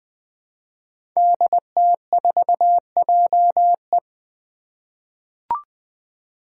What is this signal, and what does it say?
Morse code, heard as a keyed tone, about 20 wpm, 700 Hz. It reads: DT4JE
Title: Five Challenging Letter-Number Combinations 20wpm